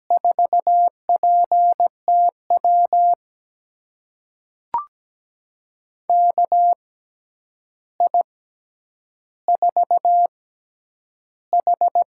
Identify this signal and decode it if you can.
Morse code, heard as a keyed tone, about 17 wpm, spaced as for 9 wpm, 700 Hz. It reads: 4PTW KI4H
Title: USA Call Signs - 6x Character Spacing - 17wpm